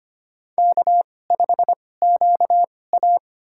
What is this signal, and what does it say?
Morse code, heard as a keyed tone, about 25 wpm, spaced as for 18 wpm, 700 Hz. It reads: K5QA
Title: USA Call Signs - 4 Characters - 2x Character Spacing - 25wpm